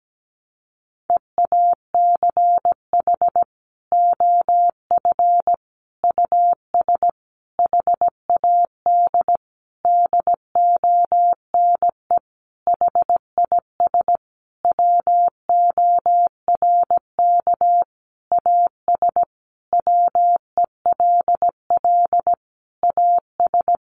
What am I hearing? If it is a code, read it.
Morse code, heard as a keyed tone, about 17 wpm, 700 Hz. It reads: EACH OF US HAD DONE HIS WORK AS WELL AS
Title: Sentences from Top 500 Words 17wpm